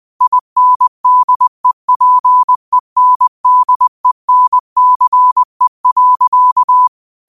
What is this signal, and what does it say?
Morse code, heard as a keyed tone, about 20 wpm, 1000 Hz. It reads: INDEPENDENCE.